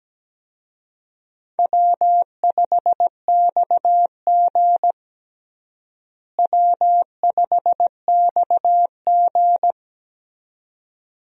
Morse code, heard as a keyed tone, about 17 wpm, 700 Hz. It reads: W5XG W5XG
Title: USA Call Signs - 4 Characters - Repeated 2x - 17wpm